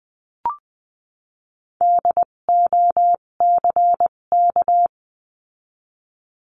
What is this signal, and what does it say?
Morse code, heard as a keyed tone, about 20 wpm, spaced as for 17 wpm, 700 Hz. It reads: DOCK
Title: 257 Lesson 42b - Words 14-20wpm